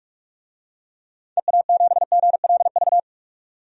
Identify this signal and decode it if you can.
Morse code, heard as a keyed tone, about 46 wpm, 700 Hz. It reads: EA7GLV
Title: DXCC-100 Call Signs - 6 Characters - 45wpm